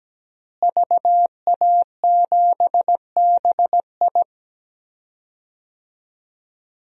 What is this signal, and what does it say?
Morse code, heard as a keyed tone, about 17 wpm, 700 Hz. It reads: VA7BI
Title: DXCC-100 Call Signs - 5 Characters - Repeated 2x - 17wpm